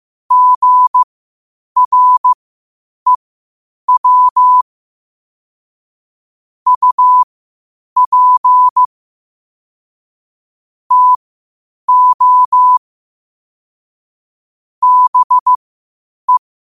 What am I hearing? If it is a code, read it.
Morse code, heard as a keyed tone, about 15 wpm, spaced as for 8 wpm, 1000 Hz. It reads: GREW UP TO BE
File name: qotdfiles / 8wpm